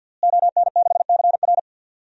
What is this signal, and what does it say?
Morse code, heard as a keyed tone, about 52 wpm, 700 Hz. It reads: ON6CR
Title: DXCC-100 Call Signs - 5 Characters - Repeated 2x - 50wpm